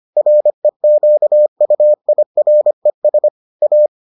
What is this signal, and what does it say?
Morse code, heard as a keyed tone, about 25 wpm, 600 Hz. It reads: REQUIRES A